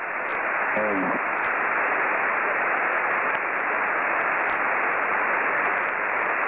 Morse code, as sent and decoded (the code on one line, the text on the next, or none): none